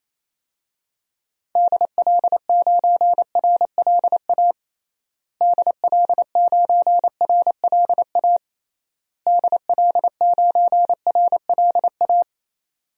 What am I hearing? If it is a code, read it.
Morse code, heard as a keyed tone, about 28 wpm, 700 Hz. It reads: DL9RLA DL9RLA DL9RLA
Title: DXCC-100 Call Signs - 6 Characters - Repeated 3x - 28wpm